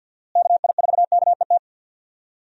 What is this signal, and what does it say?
Morse code, heard as a keyed tone, about 51 wpm, 700 Hz. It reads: KI4KET
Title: USA Call Signs - 6 Characters - Repeated 3x - 50wpm